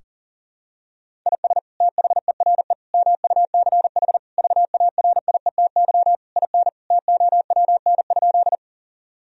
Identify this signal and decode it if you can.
Morse code, heard as a keyed tone, about 40 wpm, 700 Hz. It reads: IS THERE MUCH VARIETY IN TOWN?
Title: Sentences from Top 1000 Words - Review Part 4 - 40wpm